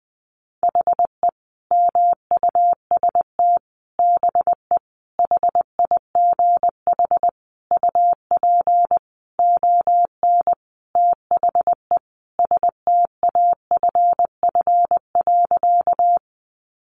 Morse code, heard as a keyed tone, about 20 wpm, 700 Hz. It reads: HE MUST BE HIGH UP ON THE STAFF.
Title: Sentences from Top 1200 Words 20wpm